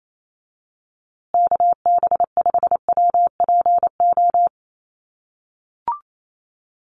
Morse code, heard as a keyed tone, about 28 wpm, 700 Hz. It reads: KB5WPO
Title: USA Call Signs Repeated 2x 28wpm